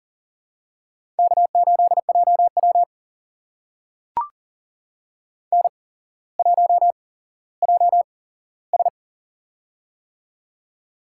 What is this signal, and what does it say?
Morse code, heard as a keyed tone, about 41 wpm, spaced as for 17 wpm, 700 Hz. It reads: K8JW N1JS
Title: USA Call Signs - 4 Characters - 8x Character Spacing - 40wpm